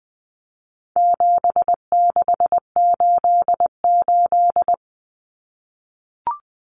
Morse code, heard as a keyed tone, about 20 wpm, 700 Hz. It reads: 7688
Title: Common Cut Numbers - 4-Digit 20wpm